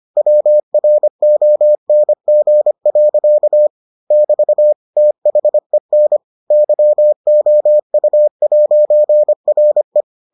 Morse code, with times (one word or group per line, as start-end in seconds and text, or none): 0.2-3.7 WRONG.
4.1-4.7 <BT>
5.0-6.2 THEN
6.5-10.0 YOU'RE